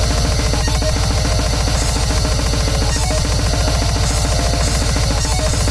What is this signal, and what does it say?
There is no code.